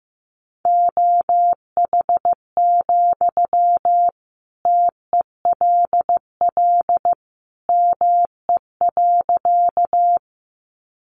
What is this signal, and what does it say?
Morse code, heard as a keyed tone, about 15 wpm, 700 Hz. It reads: OH, TELL ME.
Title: Sentences from Top 500 Words 15wpm